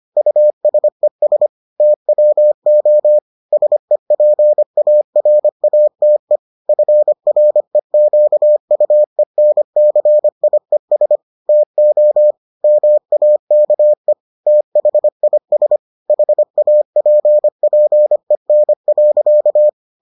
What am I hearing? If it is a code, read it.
Morse code, heard as a keyed tone, about 25 wpm, 600 Hz. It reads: USES TWO SEPARATE FREQUENCIES TO MAKE THIS HAPPEN.